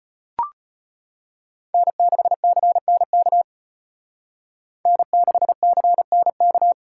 Morse code, heard as a keyed tone, about 33 wpm, 700 Hz. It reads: N6CNK N6CNK